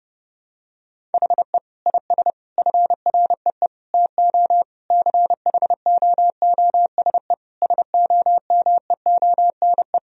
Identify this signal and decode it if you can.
Morse code, heard as a keyed tone, about 30 wpm, 700 Hz. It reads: HE IS FREE TO CHOOSE SOMEONE
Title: Sentences from Top 1000 Words - Review Part 4 - 30wpm